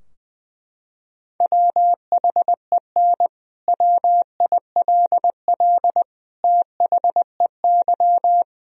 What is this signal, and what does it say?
Morse code, heard as a keyed tone, about 20 wpm, 700 Hz. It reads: WHEN WILL THEY
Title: Sentences from Top 1000 Words - Review Part 4 - 20wpm